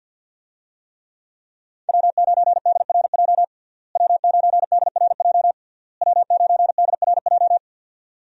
Morse code, heard as a keyed tone, about 53 wpm, 700 Hz. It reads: W9DRJ W9DRJ W9DRJ
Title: USA Call Signs - 5 Characters - Repeated 3x - 50wpm